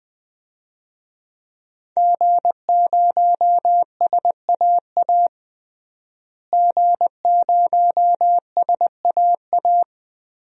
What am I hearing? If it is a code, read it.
Morse code, heard as a keyed tone, about 20 wpm, 700 Hz. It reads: G0SAA G0SAA
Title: DXCC-100 Call Signs - 5 Characters - Repeated 2x - 20wpm